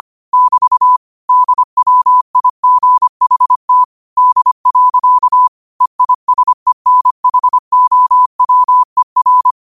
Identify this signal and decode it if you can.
Morse code, heard as a keyed tone, about 25 wpm, 1000 Hz. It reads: <BT> DWIGHT D. EISENHOWER
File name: qotdfiles / 25wpm